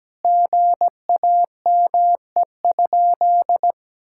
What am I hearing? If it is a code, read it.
Morse code, heard as a keyed tone, about 17 wpm, 700 Hz. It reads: GAME?
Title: Sentences from Top 1600 Words 17wpm